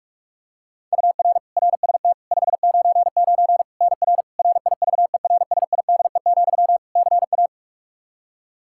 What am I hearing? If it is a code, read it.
Morse code, heard as a keyed tone, about 46 wpm, 700 Hz. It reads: UR RST 599 NR RIVERSIDE, CA
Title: Practice QSO - Exchange 1 - Beginning - 45wpm